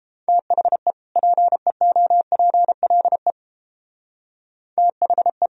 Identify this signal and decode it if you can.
Morse code, heard as a keyed tone, about 29 wpm, 700 Hz. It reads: THE PEOPLE THE